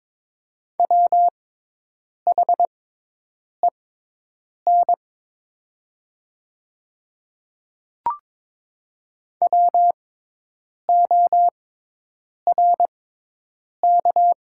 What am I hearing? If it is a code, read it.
Morse code, heard as a keyed tone, about 23 wpm, spaced as for 8 wpm, 700 Hz. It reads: WHEN WORK
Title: 6x Character Spacing - Top 300 Words - 22wpm